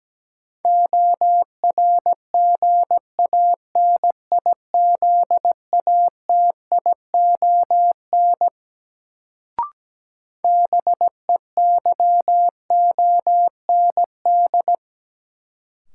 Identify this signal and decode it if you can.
Morse code, heard as a keyed tone, about 17 wpm, 700 Hz. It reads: ORGANIZATION BEYOND
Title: Sentences from Top 1100 Words - New Word Drill Alt - 17wpm